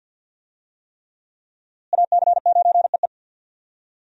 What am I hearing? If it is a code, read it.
Morse code, heard as a keyed tone, about 48 wpm, 700 Hz. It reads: AC9EE
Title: USA Call Signs - 5 Characters - Repeated 3x - 50wpm